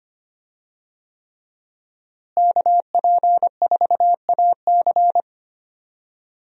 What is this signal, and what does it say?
Morse code, heard as a keyed tone, about 25 wpm, 700 Hz. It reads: KP4AC
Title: USA Call Signs - 5 Characters - Repeated 3x - 25wpm